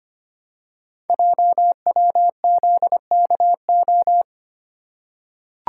Morse code, heard as a keyed tone, about 25 wpm, 700 Hz. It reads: JWZKO
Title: Code Groups - 5 Letters - 25wpm